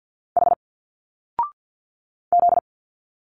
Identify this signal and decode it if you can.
Morse code, heard as a keyed tone, about 50 wpm, 700 Hz. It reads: H Z
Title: Single Letters 50wpm